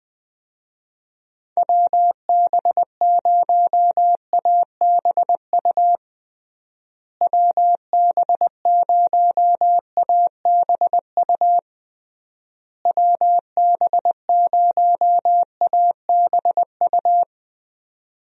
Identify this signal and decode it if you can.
Morse code, heard as a keyed tone, about 21 wpm, 700 Hz. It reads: WB0ABU WB0ABU WB0ABU
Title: USA Call Signs - 6 Characters - Repeated 3x - 20wpm